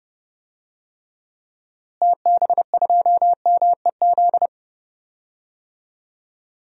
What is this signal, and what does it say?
Morse code, heard as a keyed tone, about 30 wpm, 700 Hz. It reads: TB2MEZ
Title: DXCC-100 Call Signs - 6 Characters - Repeated 3x - 30wpm